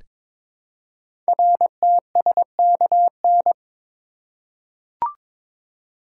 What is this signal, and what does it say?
Morse code, heard as a keyed tone, about 22 wpm, 700 Hz. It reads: RTSKN
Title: Code Groups - 5 Letters - 22wpm